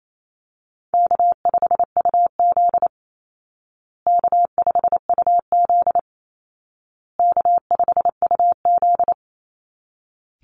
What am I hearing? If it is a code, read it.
Morse code, heard as a keyed tone, about 28 wpm, 700 Hz. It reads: K5UZ K5UZ K5UZ
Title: Easy Call Signs Repeated 3x - 28wpm